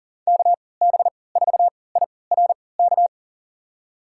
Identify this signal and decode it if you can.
Morse code, heard as a keyed tone, about 37 wpm, spaced as for 22 wpm, 700 Hz. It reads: KB4IRK
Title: USA Call Signs - 6 Characters - 3x Character Spacing - 40wpm